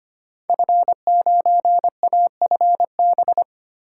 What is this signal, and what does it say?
Morse code, heard as a keyed tone, about 25 wpm, 700 Hz. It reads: F9AFB
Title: Code Groups - 5 Letter:Numbers - 25wpm